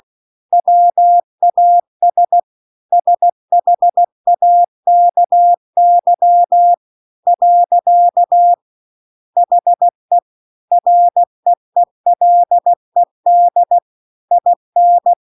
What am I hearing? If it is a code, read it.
Morse code, heard as a keyed tone, about 16 wpm, 700 Hz. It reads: WAS SHAKY . HE REELED IN
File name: CW-The-Big-Twoheaded-River-2-16WPM